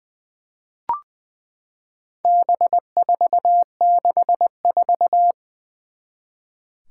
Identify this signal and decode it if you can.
Morse code, heard as a keyed tone, about 21 wpm, 700 Hz. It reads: B464
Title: Four Challenging Letter-Number Combinations 20wpm